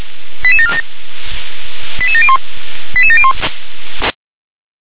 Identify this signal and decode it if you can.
Five-tone selective call (ZVEI), heard as quick runs of five tones, tones of about 70 ms. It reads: 88747 88701 88701